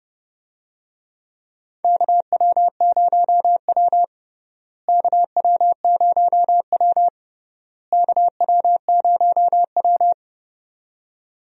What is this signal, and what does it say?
Morse code, heard as a keyed tone, about 32 wpm, 700 Hz. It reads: KW0W KW0W KW0W
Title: USA Call Signs - 4 Characters - Repeated 3x - 30wpm